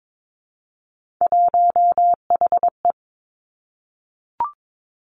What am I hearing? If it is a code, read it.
Morse code, heard as a keyed tone, about 22 wpm, 700 Hz. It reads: 1HE
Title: Three Challenging Letter-Number Combinations 22wpm